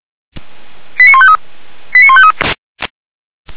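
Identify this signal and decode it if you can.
Five-tone selective call (ZVEI), heard as quick runs of five tones, tones of about 70 ms. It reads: 89153 89153
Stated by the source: ZVEI five-tone fire brigade alert received with an SDR.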